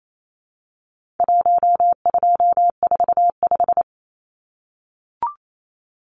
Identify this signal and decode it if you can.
Morse code, heard as a keyed tone, about 28 wpm, 700 Hz. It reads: 1245
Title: Numbers - Contest Style - 4-Digit 28wpm